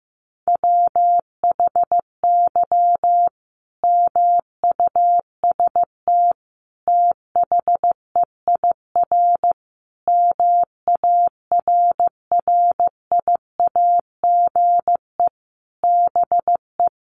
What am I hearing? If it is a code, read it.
Morse code, heard as a keyed tone, about 15 wpm, 700 Hz. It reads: WHY MUST THEIR MARRIAGE BE
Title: Sentences from Top 1000 Words 15wpm